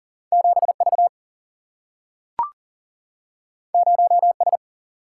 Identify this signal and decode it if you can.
Morse code, heard as a keyed tone, about 40 wpm, 700 Hz. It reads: 7V 0S
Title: Single-Letter Single-Number 40wpm